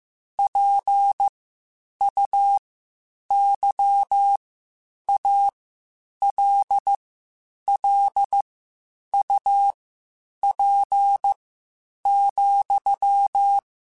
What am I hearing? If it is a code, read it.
Morse code, heard as a keyed tone, about 15 wpm, spaced as for 8 wpm, 800 Hz. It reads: PUYALLUP,
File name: (spaced) / QSOs Spaced-mp3s / WA7QOX 2 @13 sp